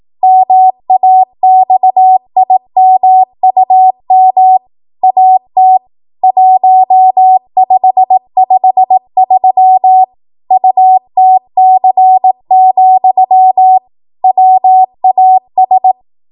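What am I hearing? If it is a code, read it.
Morse code, heard as a keyed tone, about 18 wpm, 750 Hz. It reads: MAXIMUM AT 1553 UTC, WAS
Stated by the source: ARRL 18 wpm Morse code practice archive transmission.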